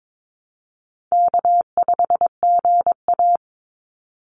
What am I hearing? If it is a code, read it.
Morse code, heard as a keyed tone, about 22 wpm, 700 Hz. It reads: K5GA
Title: Easy Call Signs - 22wpm